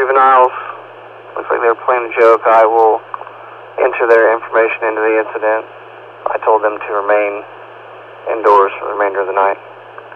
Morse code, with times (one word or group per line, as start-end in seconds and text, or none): none